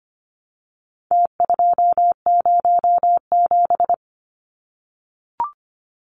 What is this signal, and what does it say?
Morse code, heard as a keyed tone, about 25 wpm, 700 Hz. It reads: T207
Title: Numbers - Contest Style - 4-Digit 25wpm